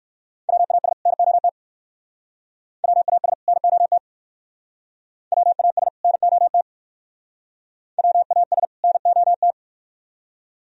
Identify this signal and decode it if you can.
Morse code, heard as a keyed tone, about 58 wpm, 700 Hz. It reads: WAS NOT WAS NOT WAS NOT WAS NOT